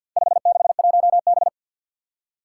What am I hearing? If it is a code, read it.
Morse code, heard as a keyed tone, about 50 wpm, 700 Hz. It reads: 561B
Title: Four Challenging Letter-Number Combinations 50wpm